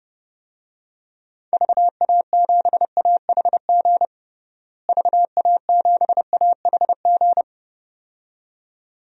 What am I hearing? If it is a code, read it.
Morse code, heard as a keyed tone, about 28 wpm, 700 Hz. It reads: VA7AHG VA7AHG
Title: DXCC-100 Call Signs - 6 Characters - Repeated 2x - 30wpm